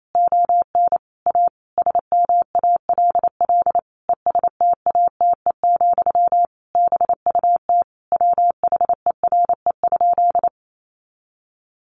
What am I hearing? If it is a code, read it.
Morse code, heard as a keyed tone, about 28 wpm, 700 Hz. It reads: ON A SMALL ESTATE, BUT WHERE?
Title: Sentences from Top 1200 Words 28wpm